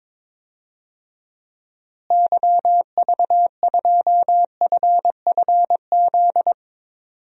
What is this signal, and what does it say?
Morse code, heard as a keyed tone, about 21 wpm, 700 Hz. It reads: YV2FFZ
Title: DXCC-100 Call Signs - 6 Characters - Repeated 3x - 22wpm